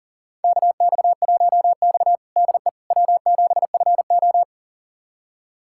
Morse code, heard as a keyed tone, about 40 wpm, 700 Hz. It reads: KX1X DE W7FO
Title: USA Callsign DE Callsign - 4 Characters - 40wpm